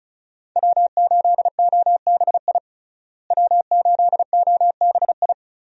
Morse code, heard as a keyed tone, about 36 wpm, 700 Hz. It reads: W8OBI W8OBI
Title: USA Call Signs - 5 Characters - Repeated 2x - 35wpm